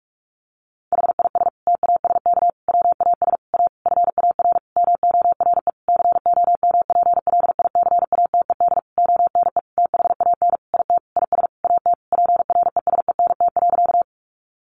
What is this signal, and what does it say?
Morse code, heard as a keyed tone, about 44 wpm, 700 Hz. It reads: HIS TASK WAS A FAR MORE COMPLICATED ONE THAN IT IS AT PRESENT.
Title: Sentences from Top 1200 Words 45wpm